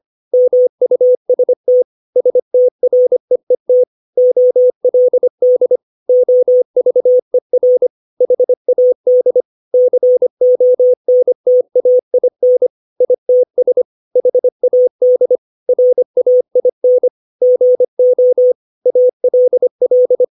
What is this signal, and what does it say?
Morse code, heard as a keyed tone, about 25 wpm, 500 Hz. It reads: MUST STREET OLD OVER HAD CONTAIN ITS HAD RAIN GO ALL
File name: 632 words / 632 Words 25WPM